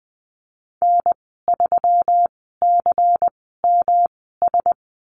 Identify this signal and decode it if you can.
Morse code, heard as a keyed tone, about 20 wpm, spaced as for 15 wpm, 700 Hz. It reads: N3CMS
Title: 128 Lesson 21g - Number 6 - Call Signs 20wpm